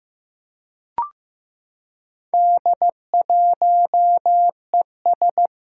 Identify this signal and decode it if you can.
Morse code, heard as a keyed tone, about 15 wpm, 700 Hz. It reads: D1ES
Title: Four Challenging Letter-Number Combinations 15wpm